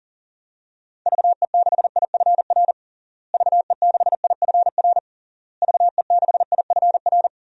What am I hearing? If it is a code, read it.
Morse code, heard as a keyed tone, about 40 wpm, 700 Hz. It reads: VE6IFR VE6IFR VE6IFR
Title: DXCC-100 Call Signs - 6 Characters - Repeated 3x - 40wpm